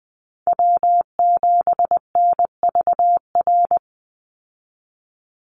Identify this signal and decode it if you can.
Morse code, heard as a keyed tone, about 20 wpm, 700 Hz. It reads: W7NVR
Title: USA Call Signs 20wpm